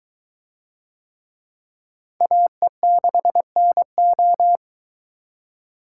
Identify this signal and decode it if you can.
Morse code, heard as a keyed tone, about 22 wpm, 700 Hz. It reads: AE6NO